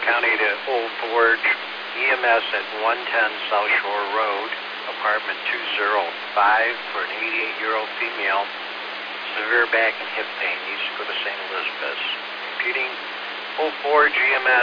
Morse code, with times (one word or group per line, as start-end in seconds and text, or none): none